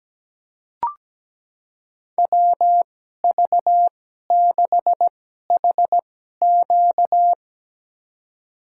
Morse code, heard as a keyed tone, about 17 wpm, spaced as for 12 wpm, 700 Hz. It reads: WV6HQ
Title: USA Call Signs - 5 Characters - 2x Character Spacing - 17wpm